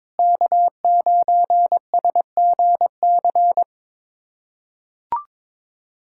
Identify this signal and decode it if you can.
Morse code, heard as a keyed tone, about 22 wpm, 700 Hz. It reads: K9SGC
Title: USA Callsign DE Callsign - 5 Characters - 22wpm